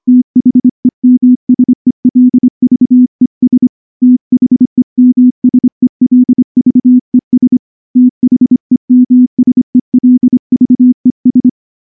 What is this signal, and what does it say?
Morse code, heard as a keyed tone, about 24 wpm, 250 Hz. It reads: THEMSELVES THEMSELVES THEMSELVES
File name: morse / morse-training-25WPM